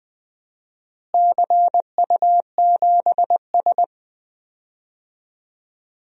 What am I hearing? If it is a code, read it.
Morse code, heard as a keyed tone, about 21 wpm, 700 Hz. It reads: CU7S
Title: DXCC-100 Call Signs - 4 Characters - Repeated 2x - 20wpm